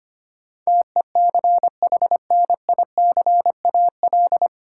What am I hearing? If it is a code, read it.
Morse code, heard as a keyed tone, about 25 wpm, 700 Hz. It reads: TECHNICAL
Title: Sentences from Top 1100 Words - New Word Drill - 25wpm